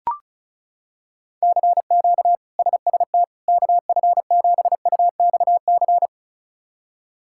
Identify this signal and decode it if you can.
Morse code, heard as a keyed tone, about 36 wpm, 700 Hz. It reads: CQ SST KF7UXC
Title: K1USN SST QSO Practice - 35wpm